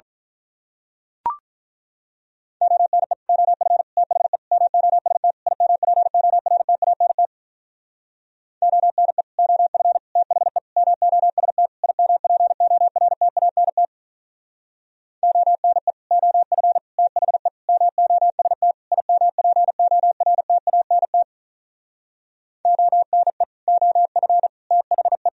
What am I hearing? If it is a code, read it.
Morse code, heard as a keyed tone, about 43 wpm, 700 Hz. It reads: ONE OF THE MOST IMPORTANT ONE OF THE MOST IMPORTANT ONE OF THE MOST IMPORTANT ONE OF THE
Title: Speed-Racing - Sets of 5 Words 35wpm